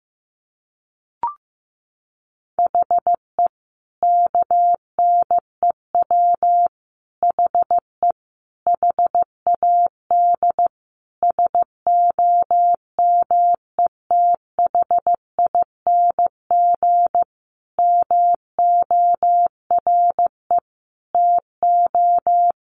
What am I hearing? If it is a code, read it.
Morse code, heard as a keyed tone, about 15 wpm, 700 Hz. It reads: HE KNEW HE HAD SOMETHING MORE TO
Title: Sentences from Top 600 Words 15wpm